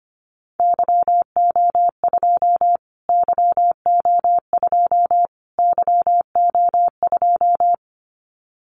Morse code, heard as a keyed tone, about 25 wpm, 700 Hz. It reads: YO2 YO2 YO2
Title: Callsign Prefixes for Top 25 DXCC - Repeated 3x - 25wpm